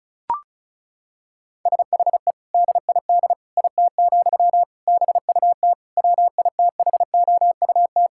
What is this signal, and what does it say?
Morse code, heard as a keyed tone, about 34 wpm, 700 Hz. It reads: SHE DID IT, BUT WITHOUT
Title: Sentences from Top 1000 Words - Review Part 6 - 35wpm